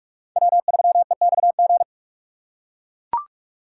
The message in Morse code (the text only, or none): W3EXG